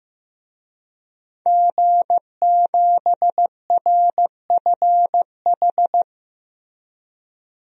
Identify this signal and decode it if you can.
Morse code, heard as a keyed tone, about 15 wpm, 700 Hz. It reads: G7RFH
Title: DXCC-100 Call Signs - 5 Characters - Repeated 2x - 15wpm